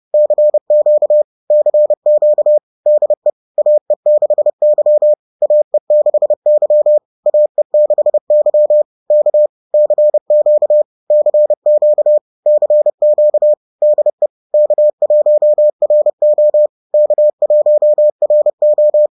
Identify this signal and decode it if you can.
Morse code, heard as a keyed tone, about 30 wpm, 600 Hz. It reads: CQ CQ DE AE6Y AE6Y AE6Y K CQ CQ CQ DE K1RO K1RO